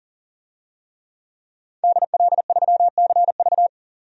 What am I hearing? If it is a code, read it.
Morse code, heard as a keyed tone, about 39 wpm, 700 Hz. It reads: DL3CV
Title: DXCC-100 Call Signs - 5 Characters - 40wpm